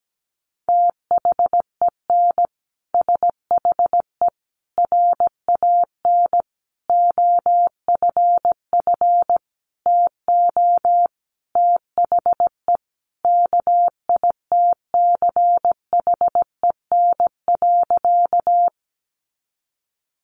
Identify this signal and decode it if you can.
Morse code, heard as a keyed tone, about 17 wpm, 700 Hz. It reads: THEN SHE RAN OFF TO THE KITCHEN.
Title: Sentences from Top 1400 Words 17wpm